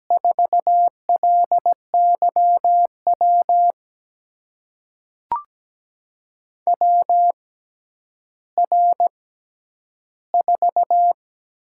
Morse code, heard as a keyed tone, about 17 wpm, spaced as for 10 wpm, 700 Hz. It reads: 4LYW WR4
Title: USA Call Signs - 6x Character Spacing - 17wpm